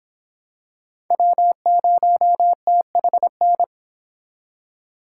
Morse code, heard as a keyed tone, about 26 wpm, 700 Hz. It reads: W0THN